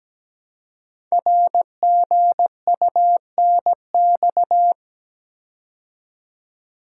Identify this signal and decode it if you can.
Morse code, heard as a keyed tone, about 17 wpm, 700 Hz. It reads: RGUNX
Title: Code Groups - 5 Letters - 17wpm